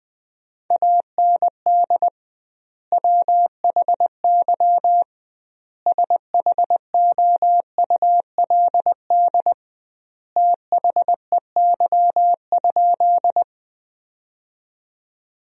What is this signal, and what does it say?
Morse code, heard as a keyed tone, about 20 wpm, 700 Hz. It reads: AND WHY SHOULD THEY?
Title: 2x Word-Spacing - Sentences from Top 300 Words - 20wpm